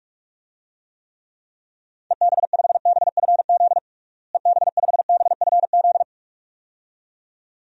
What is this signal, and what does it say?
Morse code, heard as a keyed tone, about 46 wpm, 700 Hz. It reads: EB5BFZ EB5BFZ
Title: DXCC-100 Call Signs - 6 Characters - Repeated 2x - 45wpm